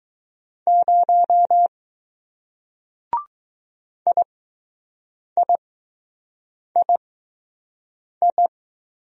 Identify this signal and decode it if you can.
Morse code, heard as a keyed tone, about 21 wpm, spaced as for 6 wpm, 700 Hz. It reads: 0 IIII